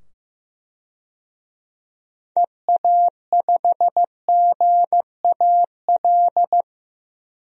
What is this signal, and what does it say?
Morse code, heard as a keyed tone, about 16 wpm, 700 Hz. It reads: EA5GAL
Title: DXCC-100 Call Signs - 6 Characters - 15wpm